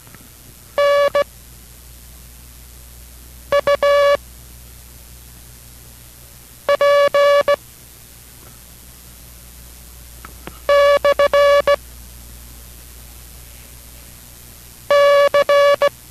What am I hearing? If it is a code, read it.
Morse code, heard as a keyed tone, about 16 wpm, spaced as for 3 wpm, 570 Hz. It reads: NUP/C